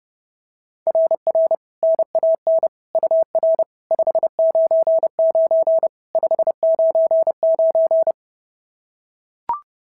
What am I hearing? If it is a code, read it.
Morse code, heard as a keyed tone, about 30 wpm, 660 Hz. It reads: RR NAN UR 599 599
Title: SOTA QSO Practice 30wpm